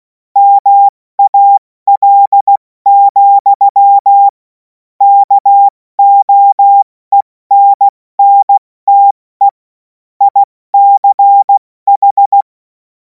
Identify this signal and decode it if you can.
Morse code, heard as a keyed tone, about 16 wpm, spaced as for 14 wpm, 800 Hz. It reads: MAL, KOENNTE ICH
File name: Morsekurs / Witze / Witz 21. April 2026 70 F80